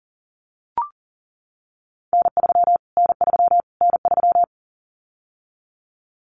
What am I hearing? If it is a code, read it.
Morse code, heard as a keyed tone, about 40 wpm, 700 Hz. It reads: N3 N3 N3
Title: Callsign Prefixes for Top 25 DXCC - Repeated 3x - 40wpm